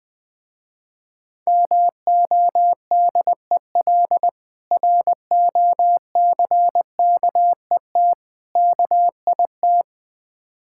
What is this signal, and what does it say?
Morse code, heard as a keyed tone, about 20 wpm, 700 Hz. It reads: MODEL ROCKET KIT
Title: Difficult Words with Hints - 20wpm